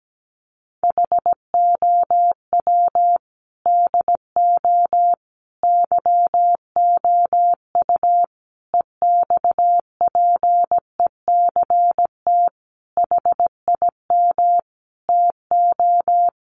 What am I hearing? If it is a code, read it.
Morse code, heard as a keyed tone, about 17 wpm, 700 Hz. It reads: HOW DO YOU EXPECT HIM TO
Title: Sentences from Top 1100 Words 17wpm